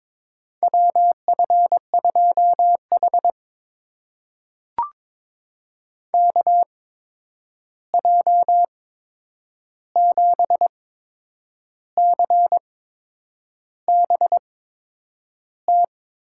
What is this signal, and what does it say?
Morse code, heard as a keyed tone, about 22 wpm, spaced as for 8 wpm, 700 Hz. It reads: WF2H KJ7CBT
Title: USA Call Signs - 8x Character Spacing - 22wpm